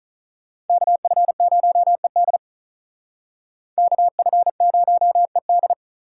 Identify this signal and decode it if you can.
Morse code, heard as a keyed tone, about 45 wpm, 700 Hz. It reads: KF0ED KF0ED